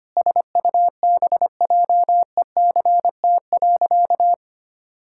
Morse code, heard as a keyed tone, about 25 wpm, 700 Hz. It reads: SUBJECT.
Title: Sentences from Top 1600 Words 25wpm